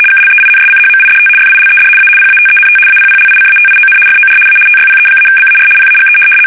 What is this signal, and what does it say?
RTTY, about 75 baud. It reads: KLEINSCHMIDT FILED A PATENT APPLICATION FOR A TYPE-BAR PAGE
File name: fsk850-75B / teletype-corp